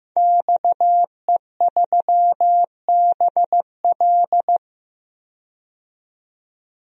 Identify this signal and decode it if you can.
Morse code, heard as a keyed tone, about 15 wpm, 700 Hz. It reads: XE3BL
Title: DXCC-100 Call Signs - 5 Characters - 15wpm